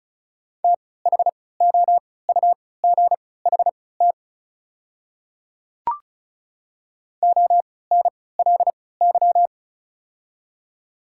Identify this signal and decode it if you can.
Morse code, heard as a keyed tone, about 36 wpm, spaced as for 20 wpm, 700 Hz. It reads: THOUGHT ONLY
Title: 3x Character Spacing - Top 300 Words - 35wpm